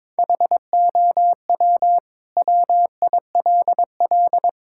Morse code, heard as a keyed tone, about 21 wpm, 700 Hz. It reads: HOW WILL
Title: Sentences from Top 1000 Words - Review Part 6 - 22wpm